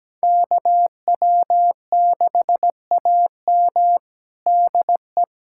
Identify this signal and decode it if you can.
Morse code, heard as a keyed tone, about 17 wpm, 700 Hz. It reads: KW6AM DE
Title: USA Callsign DE Callsign - 5 Characters - 17wpm